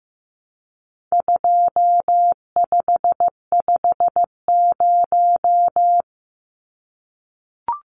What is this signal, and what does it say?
Morse code, heard as a keyed tone, about 15 wpm, 700 Hz. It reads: 2550
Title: Numbers - 4-Digit 15wpm